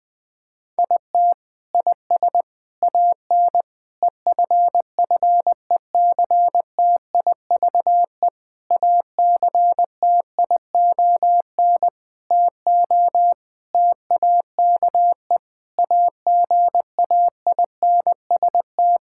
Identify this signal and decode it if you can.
Morse code, heard as a keyed tone, about 20 wpm, 700 Hz. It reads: IT IS AN EFFECTIVE ACTION TO TAKE AGAINST
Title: Sentences from Top 1000 Words - Review Part 6 - 20wpm